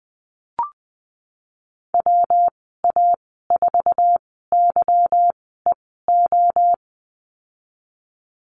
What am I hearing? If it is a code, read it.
Morse code, heard as a keyed tone, about 19 wpm, 700 Hz. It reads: W A 4 Y E O
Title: 154 Lesson 25g - Letter p - Call Signs 20wpm